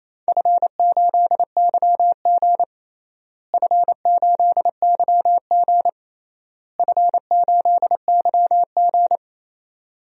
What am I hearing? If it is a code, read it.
Morse code, heard as a keyed tone, about 28 wpm, 700 Hz. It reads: F8YG F8YG F8YG
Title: DXCC-100 Call Signs - 4 Characters - Repeated 3x - 28wpm